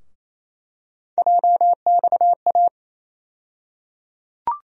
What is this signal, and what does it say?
Morse code, heard as a keyed tone, about 28 wpm, 700 Hz. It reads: JXA
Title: USA Call Sign Postfixes - 3 Letters 28wpm